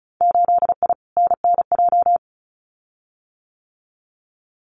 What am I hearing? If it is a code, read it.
Morse code, heard as a keyed tone, about 36 wpm, 700 Hz. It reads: 8I NNJ
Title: Winter Field Day Exchanges Encore 35wpm